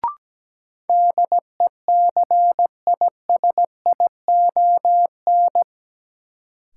Morse code, heard as a keyed tone, about 17 wpm, 700 Hz. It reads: DECISION
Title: Sentences from Top 1000 Words - New Word Drill - 17wpm